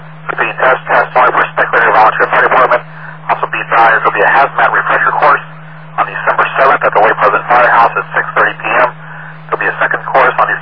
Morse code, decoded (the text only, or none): none